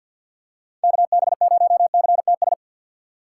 Morse code, heard as a keyed tone, about 46 wpm, 700 Hz. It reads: KB0CTS
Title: USA Call Signs - 6 Characters - 4x Character Spacing - 50wpm